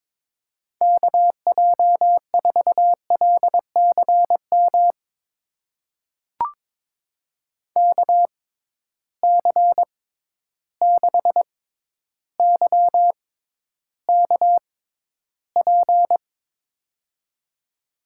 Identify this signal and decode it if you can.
Morse code, heard as a keyed tone, about 22 wpm, spaced as for 11 wpm, 700 Hz. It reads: KJ4LCM KC6YKP
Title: USA Call Signs - 6x Character Spacing - 22wpm